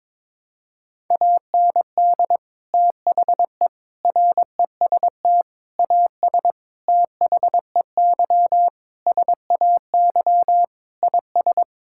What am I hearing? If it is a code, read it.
Morse code, heard as a keyed tone, about 22 wpm, 700 Hz. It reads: AND THE REST AS THEY SAY IS
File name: practice / 22wpm / Sentences with 8-Word Phrases - 22wpm